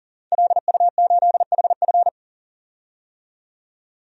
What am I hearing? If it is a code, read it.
Morse code, heard as a keyed tone, about 40 wpm, 700 Hz. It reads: LU8HF
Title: DXCC-100 Call Signs - 5 Characters - 40wpm